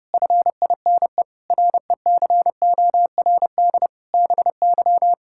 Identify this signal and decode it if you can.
Morse code, heard as a keyed tone, about 30 wpm, 700 Hz. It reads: FINE RECORD BY
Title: Pangram Challenge v2 - 30wpm